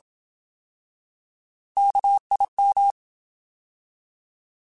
Morse code, heard as a keyed tone, about 27 wpm, 800 Hz. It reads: KIM
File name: Alphabetical / Alphabetical mp3s / Ks @25